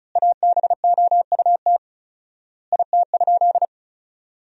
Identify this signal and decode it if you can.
Morse code, heard as a keyed tone, about 34 wpm, 700 Hz. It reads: ABOUT IT?
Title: 4x Word-Spacing - Sentences from Top 100 Words - 35wpm